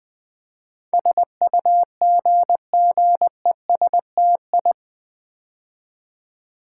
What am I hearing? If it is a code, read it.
Morse code, heard as a keyed tone, about 20 wpm, 700 Hz. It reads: SUGGESTI
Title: Build A Word - 4 Letters At A Time - 10-Letter Words - 20wpm